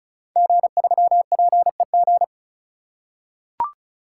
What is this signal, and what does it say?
Morse code, heard as a keyed tone, about 36 wpm, 690 Hz. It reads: G3PEG
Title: Code Groups - 5 Letter:Numbers - 35wpm